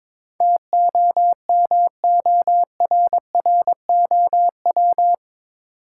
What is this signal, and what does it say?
Morse code, heard as a keyed tone, about 22 wpm, 700 Hz. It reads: TOMORROW
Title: Sentences from Top 1100 Words - New Word Drill - 22wpm